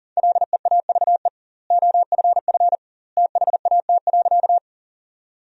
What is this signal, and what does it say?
Morse code, heard as a keyed tone, about 40 wpm, 700 Hz. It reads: LEAVE OFF THAT.
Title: 2x Word-Spacing - Sentences from Top 400 Words - 40wpm